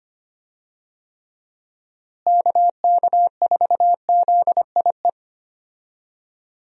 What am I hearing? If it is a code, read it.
Morse code, heard as a keyed tone, about 25 wpm, 700 Hz. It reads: KK4ZIE
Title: USA Call Signs - 6 Characters - Repeated 3x - 25wpm